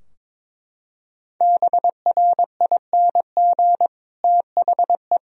Morse code, heard as a keyed tone, about 22 wpm, 700 Hz. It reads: BRING THE
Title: Sentences from Top 1000 Words - Review Part 2 - 22wpm